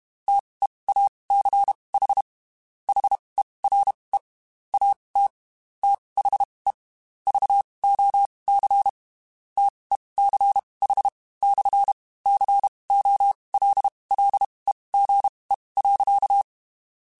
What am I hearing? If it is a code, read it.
Morse code, heard as a keyed tone, about 32 wpm, spaced as for 23 wpm, 800 Hz. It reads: TEACH HERE AT THE VOC TECH/COLLEGE.
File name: QSOs / QSOs-mp3s / WD0BGP 1@30